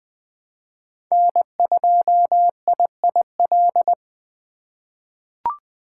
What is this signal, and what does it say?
Morse code, heard as a keyed tone, about 20 wpm, 700 Hz. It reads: N2IIL
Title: USA Call Signs - 5 Characters - 2x Character Spacing - 20wpm